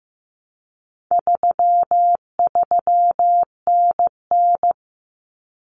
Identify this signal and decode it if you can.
Morse code, heard as a keyed tone, about 15 wpm, 700 Hz. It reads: 33NN
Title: Common Cut Numbers - 4-Digit 15wpm